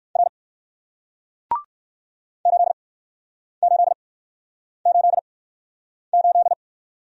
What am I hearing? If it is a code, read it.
Morse code, heard as a keyed tone, about 63 wpm, spaced as for 9 wpm, 700 Hz. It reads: R 8888